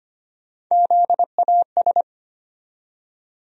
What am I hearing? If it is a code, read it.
Morse code, heard as a keyed tone, about 25 wpm, 700 Hz. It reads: ZAS
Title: USA Call Sign Postfixes - 3 Letters 25wpm